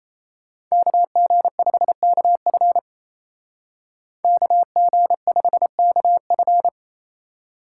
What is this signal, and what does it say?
Morse code, heard as a keyed tone, about 36 wpm, 700 Hz. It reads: KG5KF KG5KF